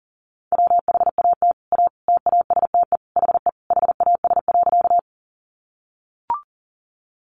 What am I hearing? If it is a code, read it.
Morse code, heard as a keyed tone, about 40 wpm, 700 Hz. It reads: WHAT A TASTE HE HAS.
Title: Sentences from Top 1100 Words 40wpm